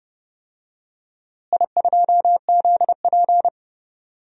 Morse code, heard as a keyed tone, about 30 wpm, 700 Hz. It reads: I2ZP
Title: DXCC-100 Call Signs - 4 Characters - Repeated 3x - 30wpm